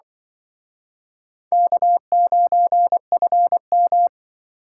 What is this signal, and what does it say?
Morse code, heard as a keyed tone, about 25 wpm, 700 Hz. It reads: K9FM